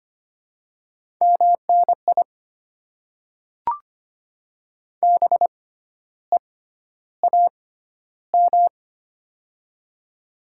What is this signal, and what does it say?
Morse code, heard as a keyed tone, about 25 wpm, spaced as for 12 wpm, 700 Hz. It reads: MNI BEAM
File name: Top 90 QSO Elements - 6x Character Spacing - 25wpm